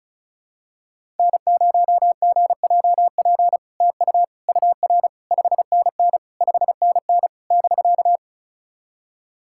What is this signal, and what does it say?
Morse code, heard as a keyed tone, about 36 wpm, 700 Hz. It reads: N0GJP TU UR 5NN 5NN <BK>
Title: SOTA QSO Practice 35wpm